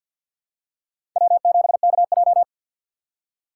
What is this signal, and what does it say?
Morse code, heard as a keyed tone, about 53 wpm, 700 Hz. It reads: W7KJ
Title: USA Call Signs - 2x Character Spacing - 50wpm